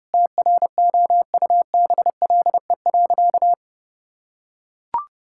Morse code, heard as a keyed tone, about 30 wpm, 700 Hz. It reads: TROUBLE.
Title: Sentences from Top 1600 Words 30wpm